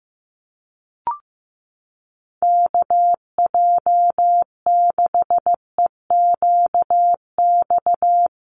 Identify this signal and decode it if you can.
Morse code, heard as a keyed tone, about 15 wpm, 700 Hz. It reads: KJ6EQX
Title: USA Call Signs Repeated 2x 15wpm